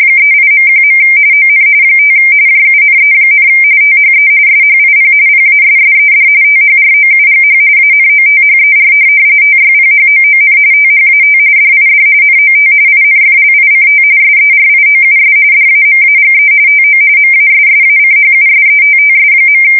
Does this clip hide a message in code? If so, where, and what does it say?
Yes: RTTY from 0.2 to 19.6 s, about 45.45 baud. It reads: 38.1. ON 5 JUNE, THE SHIP PASSED THROUGH ANOTHER TYPHOON THAT DID NOT INFLICT SERIOUS DAMAGE TO MASSACHUSETTS.